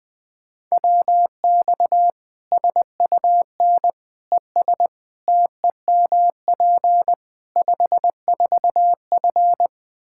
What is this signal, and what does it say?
Morse code, heard as a keyed tone, about 20 wpm, 700 Hz. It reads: WX SUN ES TEMP 54F
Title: Practice QSO - Exchange 2 - Info - 20wpm